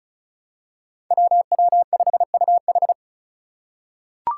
Morse code, heard as a keyed tone, about 36 wpm, 700 Hz. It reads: WW5UH
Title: Five Challenging Letter-Number Combinations 35wpm